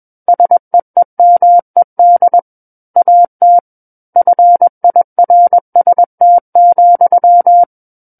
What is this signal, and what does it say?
Morse code, heard as a keyed tone, about 21 wpm, 700 Hz. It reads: SEEMED AT FIRST,
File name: Book One / 12-Chapter 12 - 21 WPM